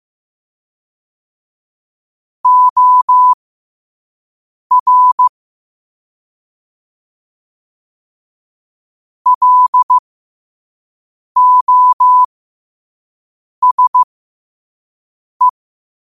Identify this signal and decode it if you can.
Morse code, heard as a keyed tone, about 15 wpm, spaced as for 5 wpm, 1000 Hz. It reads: OR LOSE
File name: qotdfiles / 5wpm